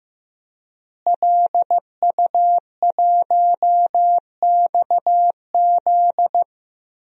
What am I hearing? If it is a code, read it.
Morse code, heard as a keyed tone, about 15 wpm, 700 Hz. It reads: LU1XZ
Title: DXCC-100 Call Signs - 5 Characters - Repeated 2x - 15wpm